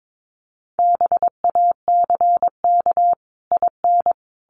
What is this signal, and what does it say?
Morse code, heard as a keyed tone, about 22 wpm, 700 Hz. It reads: BACK IN
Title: Sets of 5 Words Encore 22wpm